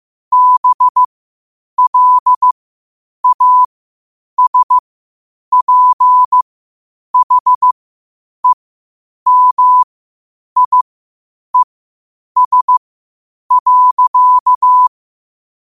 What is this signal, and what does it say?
Morse code, heard as a keyed tone, about 15 wpm, spaced as for 8 wpm, 1000 Hz. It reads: BLASPHEMIES.
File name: qotdfiles / 8wpm